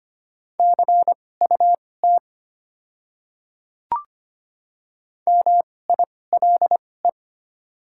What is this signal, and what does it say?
Morse code, heard as a keyed tone, about 25 wpm, spaced as for 18 wpm, 700 Hz. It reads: CUT MILE
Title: 2x Character Spacing - Top 300 Words - 25wpm